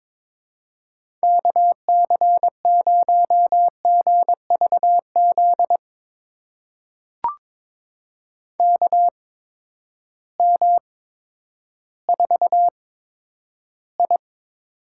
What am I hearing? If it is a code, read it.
Morse code, heard as a keyed tone, about 22 wpm, spaced as for 11 wpm, 700 Hz. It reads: KC0GVZ KM4I
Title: USA Call Signs - 6 Characters - 8x Character Spacing - 22wpm